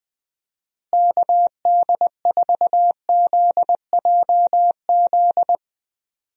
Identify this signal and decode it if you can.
Morse code, heard as a keyed tone, about 20 wpm, 700 Hz. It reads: KD4ZJZ
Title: USA Call Signs - 6 Characters - Repeated 2x - 20wpm